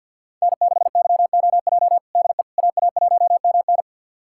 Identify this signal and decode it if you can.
Morse code, heard as a keyed tone, about 46 wpm, 700 Hz. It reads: N6YOJ DE AA1MN
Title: USA Callsign DE Callsign - 5 Characters - 50wpm